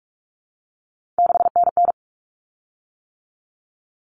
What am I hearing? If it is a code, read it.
Morse code, heard as a keyed tone, about 44 wpm, 700 Hz. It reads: 6NN
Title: Common Cut Numbers - 3-Digit 45wpm